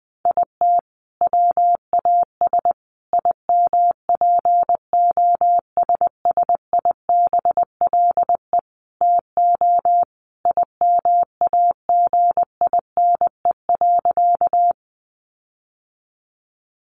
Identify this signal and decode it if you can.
Morse code, heard as a keyed tone, about 20 wpm, 700 Hz. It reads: IT WAS IMPOSSIBLE TO IMAGINE.
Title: Sentences from Top 1400 Words 20wpm